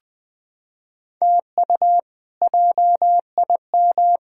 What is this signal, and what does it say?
Morse code, heard as a keyed tone, about 21 wpm, 700 Hz. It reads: TU JIM
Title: K1USN SST QSO Practice - 20wpm